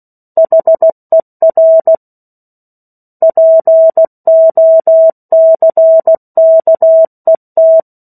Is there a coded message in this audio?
Yes: Morse code from 0.4 to 7.8 s, heard as a keyed tone, about 16 wpm, 650 Hz. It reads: HER POCKET